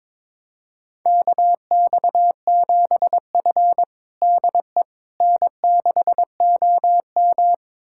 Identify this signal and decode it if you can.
Morse code, heard as a keyed tone, about 22 wpm, 700 Hz. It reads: KX7F DE N6OM
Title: USA Callsign DE Callsign - 4 Characters - 22wpm